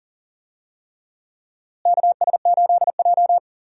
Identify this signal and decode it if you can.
Morse code, heard as a keyed tone, about 40 wpm, 700 Hz. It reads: KS8J
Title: USA Call Signs - 4 Characters - Repeated 3x - 40wpm